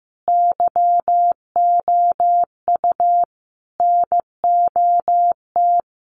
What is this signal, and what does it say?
Morse code, heard as a keyed tone, about 15 wpm, 700 Hz. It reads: YOU NOT
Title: Sentences from Top 1100 Words 15wpm